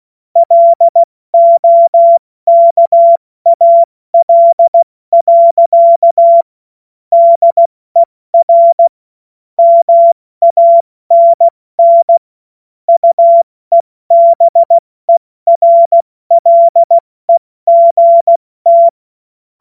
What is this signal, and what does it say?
Morse code, heard as a keyed tone, about 16 wpm, 680 Hz. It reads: LOKAL. DER MANN UEBERLEGT
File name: Morsekurs / Witze / Witz 11. Mai 2021, 70 F80